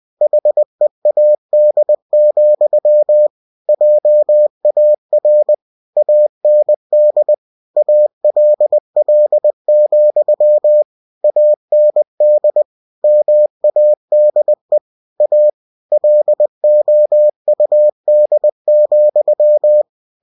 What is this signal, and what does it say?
Morse code, heard as a keyed tone, about 20 wpm, 600 Hz. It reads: HEAD, JAR AND ALL, AND MADE A LOUD,